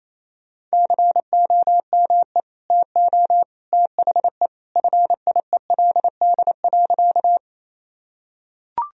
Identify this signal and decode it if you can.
Morse code, heard as a keyed tone, about 28 wpm, 700 Hz. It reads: COME TO THE FIELD.
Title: Sentences from Top 1000 Words - Review Part 2 - 28wpm